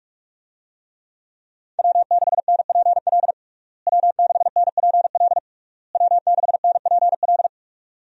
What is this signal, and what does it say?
Morse code, heard as a keyed tone, about 47 wpm, 700 Hz. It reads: W6NPL W6NPL W6NPL
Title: USA Call Signs - 5 Characters - Repeated 3x - 45wpm